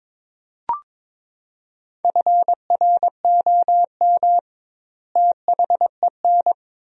Morse code, heard as a keyed tone, about 21 wpm, 700 Hz. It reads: FROM THEN
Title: 2x Word-Spacing - Sentences from Top 400 Words - 22wpm